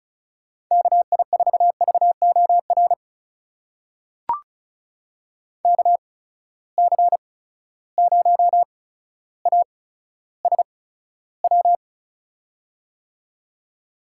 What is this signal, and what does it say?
Morse code, heard as a keyed tone, about 36 wpm, spaced as for 15 wpm, 700 Hz. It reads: KI4VOR KC0ASW
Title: USA Call Signs - 6 Characters - 8x Character Spacing - 35wpm